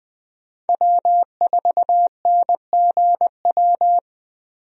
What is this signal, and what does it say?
Morse code, heard as a keyed tone, about 20 wpm, 700 Hz. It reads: W4NGW
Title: USA Call Signs - 5 Characters - 20wpm